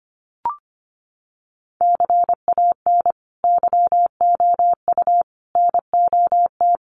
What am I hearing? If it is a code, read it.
Morse code, heard as a keyed tone, about 25 wpm, 700 Hz. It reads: CAN YOU NOT
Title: Sentences from Top 600 Words 25wpm